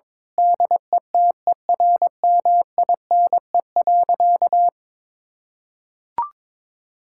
Morse code, heard as a keyed tone, about 22 wpm, 700 Hz. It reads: DETERMINE.
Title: Sentences from Top 1000 Words - Review Part 5 - 22wpm